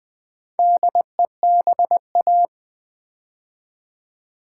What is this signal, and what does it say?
Morse code, heard as a keyed tone, about 20 wpm, 690 Hz. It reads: DEBA
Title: Build A Word - 2 Letters At A Time - 6-Letter Words - 20wpm